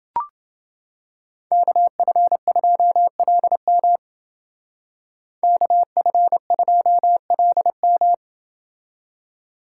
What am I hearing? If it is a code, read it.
Morse code, heard as a keyed tone, about 29 wpm, 700 Hz. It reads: KF2LM KF2LM